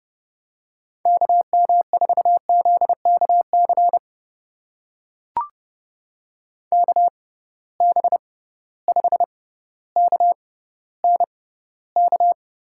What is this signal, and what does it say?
Morse code, heard as a keyed tone, about 31 wpm, spaced as for 16 wpm, 700 Hz. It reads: KM4ZKC KB5KNK
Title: USA Call Signs - 6 Characters - 6x Character Spacing - 30wpm